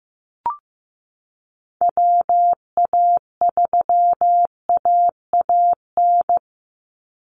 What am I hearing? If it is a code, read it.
Morse code, heard as a keyed tone, about 15 wpm, 700 Hz. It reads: WA3AAN
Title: Call Signs Repeated 2x - 15wpm